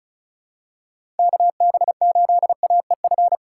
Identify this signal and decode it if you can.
Morse code, heard as a keyed tone, about 34 wpm, 700 Hz. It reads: KB8AEF
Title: USA Call Signs - 6 Characters - 2x Character Spacing - 35wpm